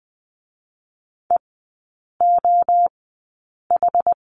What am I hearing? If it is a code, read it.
Morse code, heard as a keyed tone, about 20 wpm, 700 Hz. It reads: E O H
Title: 107 Lesson 18e - Letter m - Three Characters 20wpm